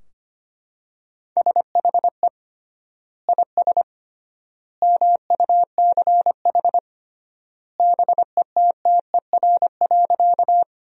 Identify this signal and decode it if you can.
Morse code, heard as a keyed tone, about 25 wpm, 700 Hz. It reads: SHE IS MUCH BETTER.
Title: 3x Word-Spacing - Sentences from Top 400 Words - 25wpm